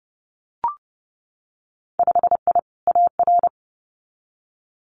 Morse code, heard as a keyed tone, about 32 wpm, 700 Hz. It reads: 5I AR
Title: Winter Field Day Exchanges Encore 30wpm